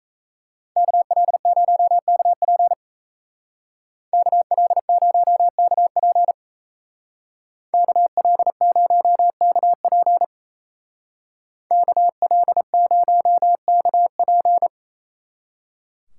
Morse code, heard as a keyed tone, about 36 wpm, 700 Hz. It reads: KL0KP KL0KP KL0KP KL0KP